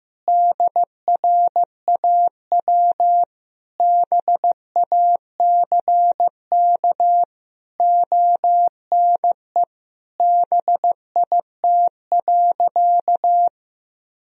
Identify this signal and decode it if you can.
Morse code, heard as a keyed tone, about 16 wpm, 700 Hz. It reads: DRAW BACK ONE BIT.
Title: Sentences from Top 1000 Words - Review Part 1 - 15wpm